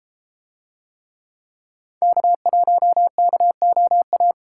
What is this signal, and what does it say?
Morse code, heard as a keyed tone, about 32 wpm, 700 Hz. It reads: K1KOA